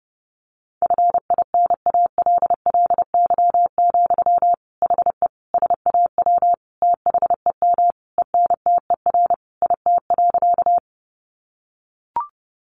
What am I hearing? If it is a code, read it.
Morse code, heard as a keyed tone, about 28 wpm, 700 Hz. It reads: FINALLY, HE SAW THEM ENTER IT.
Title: Sentences from Top 1300 Words 30wpm